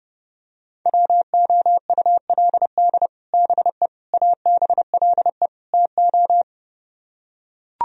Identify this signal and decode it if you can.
Morse code, heard as a keyed tone, about 31 wpm, 700 Hz. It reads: WOULD BE ABLE TO